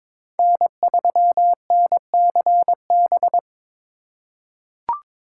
Morse code, heard as a keyed tone, about 21 wpm, 700 Hz. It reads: N3NCB
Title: USA Call Signs - 4x Character Spacing - 22wpm